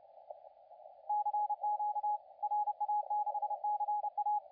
Morse code, heard as a keyed tone, about 32 wpm, 840 Hz, with radio noise. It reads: CQ RA6CA
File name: la4c / sac-cw-2017 / CONTEST AUDIO 2100Z